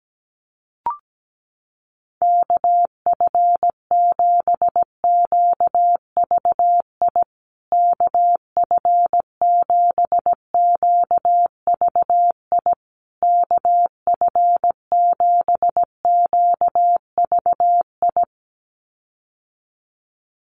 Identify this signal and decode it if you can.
Morse code, heard as a keyed tone, about 17 wpm, 700 Hz. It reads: KF7QVI KF7QVI KF7QVI
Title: USA Call Signs Repeated 3x 17wpm